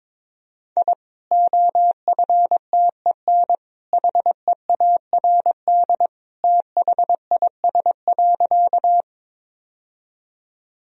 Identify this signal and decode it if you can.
Morse code, heard as a keyed tone, about 22 wpm, 700 Hz. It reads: I OFTEN HEARD THIS.
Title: Sentences from Top 1000 Words - Review Part 1 - 22wpm